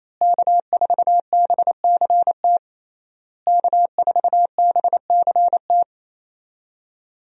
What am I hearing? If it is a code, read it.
Morse code, heard as a keyed tone, about 28 wpm, 700 Hz. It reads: K4BCT K4BCT
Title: USA Call Signs - 5 Characters - Repeated 3x - 28wpm